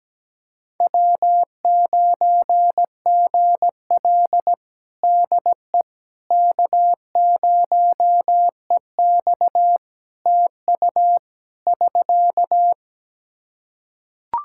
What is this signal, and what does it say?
Morse code, heard as a keyed tone, about 17 wpm, 700 Hz. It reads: W9GL DE K0EX TU <SK>
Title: Practice QSO - Exchange 4 - Ending - 17wpm